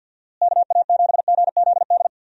Morse code, heard as a keyed tone, about 51 wpm, 700 Hz. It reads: KA7GZD
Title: USA Call Signs - 6 Characters - Repeated 2x - 50wpm